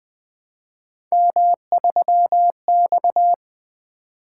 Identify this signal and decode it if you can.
Morse code, heard as a keyed tone, about 20 wpm, 700 Hz. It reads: M3X
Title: Three Letter-Number Combinations - 20wpm